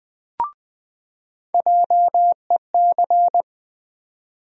Morse code, heard as a keyed tone, about 20 wpm, 700 Hz. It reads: JEC
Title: USA Call Sign Postfixes - 3 Letters 20wpm